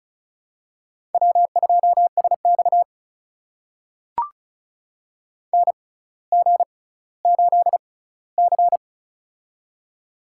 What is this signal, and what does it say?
Morse code, heard as a keyed tone, about 36 wpm, spaced as for 18 wpm, 700 Hz. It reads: W2SX NG8C
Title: USA Call Signs - 4 Characters - 6x Character Spacing - 35wpm